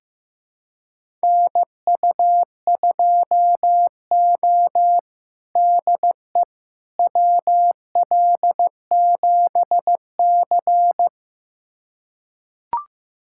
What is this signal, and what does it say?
Morse code, heard as a keyed tone, about 15 wpm, 700 Hz. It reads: NU2O DE WL7C
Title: USA Callsign DE Callsign - 4 Characters - 15wpm